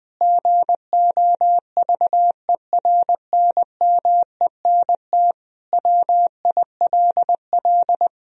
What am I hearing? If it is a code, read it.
Morse code, heard as a keyed tone, about 21 wpm, 700 Hz. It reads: GOVERNMENT WILL
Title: Sentences from Top 1000 Words - Review Part 3 - 20wpm